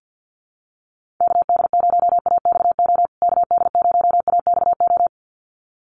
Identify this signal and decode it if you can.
Morse code, heard as a keyed tone, about 50 wpm, 700 Hz. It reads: KD0AXO KD0AXO
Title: USA Call Signs Repeated 2x 50wpm